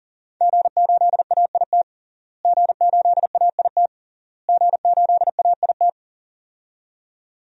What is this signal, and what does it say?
Morse code, heard as a keyed tone, about 37 wpm, 700 Hz. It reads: G8AIT G8AIT G8AIT
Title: DXCC-100 Call Signs - 5 Characters - Repeated 3x - 40wpm